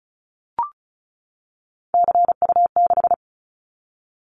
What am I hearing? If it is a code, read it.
Morse code, heard as a keyed tone, about 34 wpm, 700 Hz. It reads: CU6
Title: Callsign Prefixes for Top 100 DXCC - 35wpm